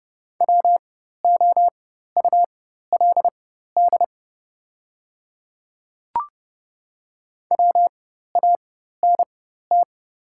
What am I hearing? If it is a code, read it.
Morse code, heard as a keyed tone, about 30 wpm, spaced as for 14 wpm, 700 Hz. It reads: WOULD WANT
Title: 4x Character Spacing - Top 200 Words - 30wpm